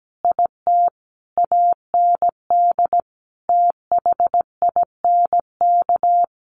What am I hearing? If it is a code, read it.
Morse code, heard as a keyed tone, about 17 wpm, 700 Hz. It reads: IT AND THINK
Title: Sentences from Top 1400 Words 17wpm